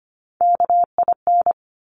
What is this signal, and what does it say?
Morse code, heard as a keyed tone, about 25 wpm, 700 Hz. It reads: KIN
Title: Top Three Letter Combinations in Words 25wpm